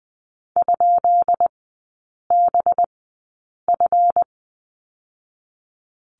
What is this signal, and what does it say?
Morse code, heard as a keyed tone, about 20 wpm, spaced as for 8 wpm, 700 Hz. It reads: ?BF
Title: 191 Lesson 31e - Letter v - Three Characters 20wpm